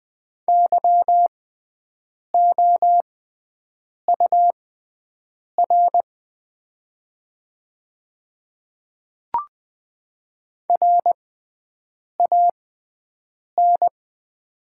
Morse code, heard as a keyed tone, about 20 wpm, spaced as for 7 wpm, 690 Hz. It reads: YOUR RAN
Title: 6x Character Spacing - Top 500 Words - 20wpm